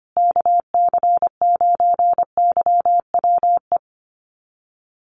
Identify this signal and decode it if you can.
Morse code, heard as a keyed tone, about 25 wpm, 700 Hz. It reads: KC9YWE
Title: USA Call Signs Repeated 3x 25wpm